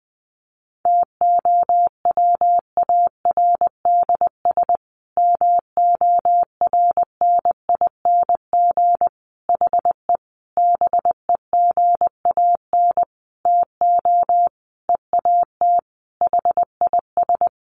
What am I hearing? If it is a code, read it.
Morse code, heard as a keyed tone, about 20 wpm, 700 Hz. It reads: TOWARDS MORNING HE BEGAN TO EAT HIS
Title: Sentences from Top 1100 Words 20wpm